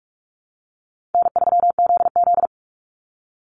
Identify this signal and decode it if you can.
Morse code, heard as a keyed tone, about 44 wpm, 700 Hz. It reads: N3ZZ
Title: Call Signs Repeated 2x - 45wpm